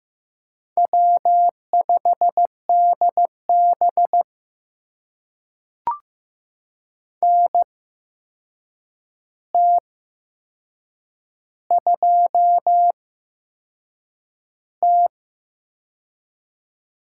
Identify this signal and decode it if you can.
Morse code, heard as a keyed tone, about 15 wpm, spaced as for 6 wpm, 700 Hz. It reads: W5DB NT2T
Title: USA Call Signs - 4 Characters - 8x Character Spacing - 15wpm